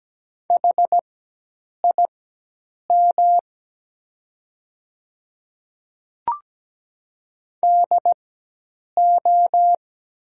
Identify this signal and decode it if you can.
Morse code, heard as a keyed tone, about 17 wpm, spaced as for 8 wpm, 700 Hz. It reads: HIM DO
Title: 4x Character Spacing - Top 100 Words - 17wpm